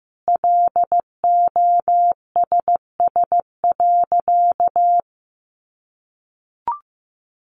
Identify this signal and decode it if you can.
Morse code, heard as a keyed tone, about 15 wpm, 700 Hz. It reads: LOSS.
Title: Sentences from Top 1400 Words 15wpm